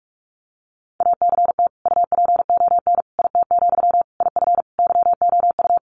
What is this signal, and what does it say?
Morse code, heard as a keyed tone, about 44 wpm, 700 Hz. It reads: ACT UPON IT, IF YOU
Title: Sentences from Top 1100 Words 45wpm